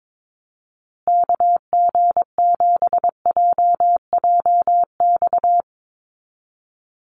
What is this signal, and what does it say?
Morse code, heard as a keyed tone, about 22 wpm, 700 Hz. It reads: KG7JJX
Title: USA Call Signs Repeated 2x 22wpm